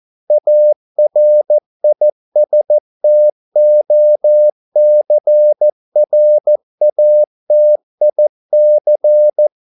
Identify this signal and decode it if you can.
Morse code, heard as a keyed tone, about 14 wpm, 600 Hz. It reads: ARISTOCRATIC
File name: mp3 files / a dogs mistake - 14wpm